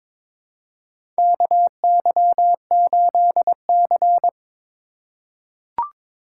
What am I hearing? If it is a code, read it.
Morse code, heard as a keyed tone, about 22 wpm, 700 Hz. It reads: KY8C
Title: USA Call Signs - 4x Character Spacing - 22wpm